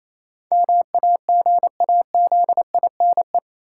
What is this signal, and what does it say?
Morse code, heard as a keyed tone, about 28 wpm, 700 Hz. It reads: MAGAZINE
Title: Sentences from Top 1000 Words - Review Part 7 - New Word Drill - 28wpm